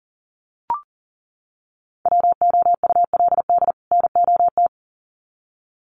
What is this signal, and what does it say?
Morse code, heard as a keyed tone, about 40 wpm, 700 Hz. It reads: WOULD NOT
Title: Sets of 2 Words 40wpm